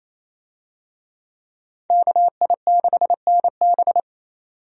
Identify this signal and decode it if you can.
Morse code, heard as a keyed tone, about 28 wpm, 700 Hz. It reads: KI6NB
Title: USA Call Signs - 5 Characters - 28wpm